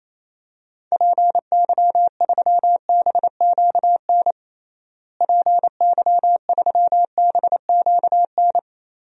Morse code, heard as a keyed tone, about 28 wpm, 700 Hz. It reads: PY3BQN PY3BQN
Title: DXCC-100 Call Signs - 6 Characters - Repeated 2x - 28wpm